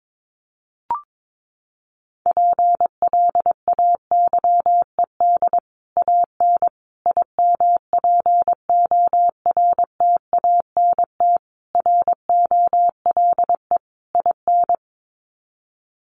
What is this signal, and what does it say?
Morse code, heard as a keyed tone, about 22 wpm, 700 Hz. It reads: PLAYED AN IMPORTANT ROLE IN
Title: Sets of 5 Words Encore2 22wpm